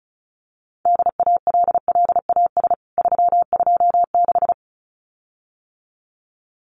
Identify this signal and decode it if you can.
Morse code, heard as a keyed tone, about 34 wpm, 700 Hz. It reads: DALLAS 326
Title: CWT Exchange 35wpm